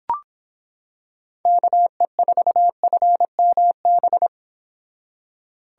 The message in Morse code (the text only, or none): KE4FMB